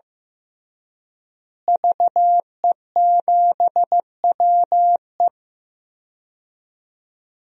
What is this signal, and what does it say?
Morse code, heard as a keyed tone, about 14 wpm, 700 Hz. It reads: VE7WE
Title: DXCC-100 Call Signs - 5 Characters - Repeated 3x - 15wpm